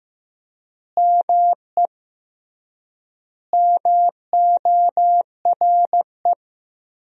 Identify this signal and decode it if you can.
Morse code, heard as a keyed tone, about 15 wpm, 700 Hz. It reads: ME MORE
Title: 3x Word-Spacing - Sentences from Top 200 Words - 15wpm